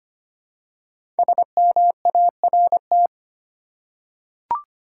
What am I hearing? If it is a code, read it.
Morse code, heard as a keyed tone, about 25 wpm, 700 Hz. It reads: SMART
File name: 5 Letter Words - First 3 Letters Given - 25wpm